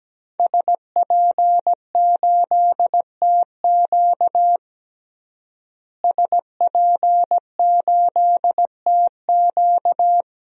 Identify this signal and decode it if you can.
Morse code, heard as a keyed tone, about 17 wpm, 700 Hz. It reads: SP8TQ SP8TQ
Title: DXCC-100 Call Signs - 5 Characters - Repeated 2x - 17wpm